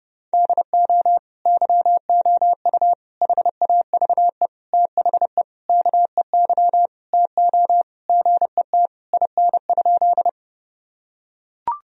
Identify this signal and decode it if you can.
Morse code, heard as a keyed tone, about 29 wpm, 700 Hz. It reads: DO YOU HAVE THE KEY TO GET IN?
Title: Sentences from Top 1000 Words - Review Part 4 - 30wpm